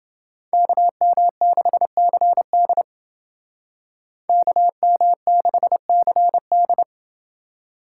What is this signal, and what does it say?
Morse code, heard as a keyed tone, about 28 wpm, 700 Hz. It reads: KM6CD KM6CD